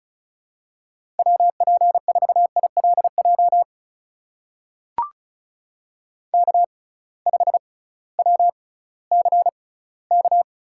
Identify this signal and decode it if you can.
Morse code, heard as a keyed tone, about 36 wpm, spaced as for 19 wpm, 700 Hz. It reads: WP4ILJ K5WCK
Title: USA Call Signs - 6x Character Spacing - 35wpm